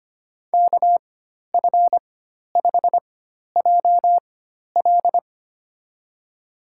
Morse code, heard as a keyed tone, about 25 wpm, spaced as for 12 wpm, 700 Hz. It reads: KF5JL
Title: USA Call Signs - 5 Characters - 4x Character Spacing - 25wpm